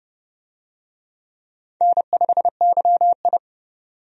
Morse code, heard as a keyed tone, about 30 wpm, 700 Hz. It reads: N5YI
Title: USA Call Signs - 4 Characters - 30wpm